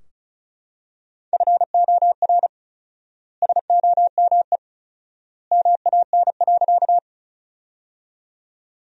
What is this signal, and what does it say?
Morse code, heard as a keyed tone, about 36 wpm, 700 Hz. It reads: FOR SOME MAN.
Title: 4x Word-Spacing - Sentences from Top 200 Words - 35wpm